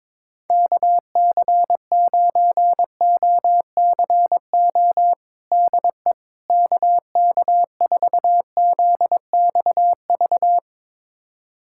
Morse code, heard as a keyed tone, about 22 wpm, 700 Hz. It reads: KC9OCO DE KK4ZXV
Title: USA Callsign DE Callsign 22wpm